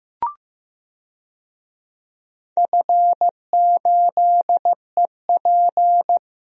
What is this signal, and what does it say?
Morse code, heard as a keyed tone, about 15 wpm, 700 Hz. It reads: F8EP
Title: DXCC-100 Call Signs - 4 Characters - 15wpm